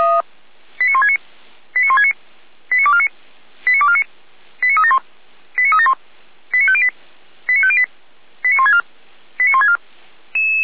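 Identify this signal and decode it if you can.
Five-tone selective call (ZVEI), heard as quick runs of five tones, tones of about 70 ms. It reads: 89169 89169 89249 89249 89371 89371 89508 89508 89164 89164
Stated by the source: ZVEI five-tone selective call received by SDR.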